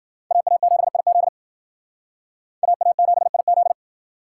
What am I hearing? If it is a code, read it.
Morse code, heard as a keyed tone, about 51 wpm, 700 Hz. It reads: AA7IZ AA7IZ